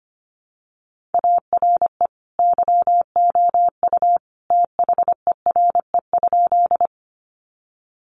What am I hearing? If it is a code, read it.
Morse code, heard as a keyed tone, about 25 wpm, 700 Hz. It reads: ARE YOU THERE?
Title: Sentences from Top 100 Words 25wpm